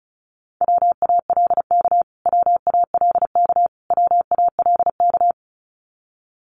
Morse code, heard as a keyed tone, about 34 wpm, 700 Hz. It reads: WALK WALK WALK
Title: Top 500 Words Repeated 35wpm